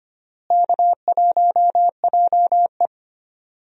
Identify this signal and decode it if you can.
Morse code, heard as a keyed tone, about 25 wpm, 700 Hz. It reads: K1JE
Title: USA Call Signs - 4 Characters - Repeated 3x - 25wpm